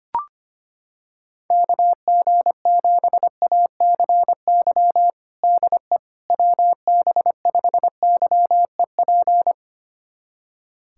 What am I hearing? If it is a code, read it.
Morse code, heard as a keyed tone, about 25 wpm, 700 Hz. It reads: KG7ACY DE WB5YEP
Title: USA Callsign DE Callsign - 6 Characters - 25wpm